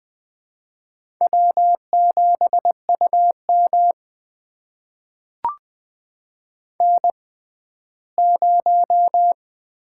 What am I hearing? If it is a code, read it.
Morse code, heard as a keyed tone, about 20 wpm, spaced as for 14 wpm, 700 Hz. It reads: W7UM N0
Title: USA Call Signs - 4 Characters - 6x Character Spacing - 20wpm